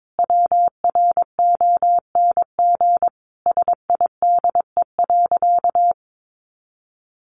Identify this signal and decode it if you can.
Morse code, heard as a keyed tone, about 22 wpm, 700 Hz. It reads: WRONG SIDE.
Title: Sentences from Top 900 Words 22wpm